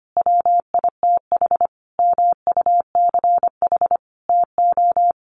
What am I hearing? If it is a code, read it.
Morse code, heard as a keyed tone, about 25 wpm, 700 Hz. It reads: WITH MUCH TO
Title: Sentences from Top 1400 Words 25wpm